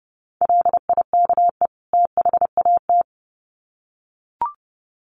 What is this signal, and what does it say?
Morse code, heard as a keyed tone, about 30 wpm, 700 Hz. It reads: LIKE THAT
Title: Sets of 2 Words 30wpm